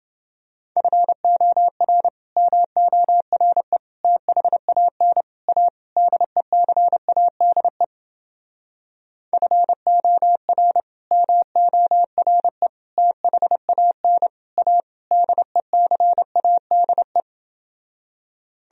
Morse code, heard as a keyed tone, about 30 wpm, 700 Hz. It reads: FOR MORE THAN A DECADE FOR MORE THAN A DECADE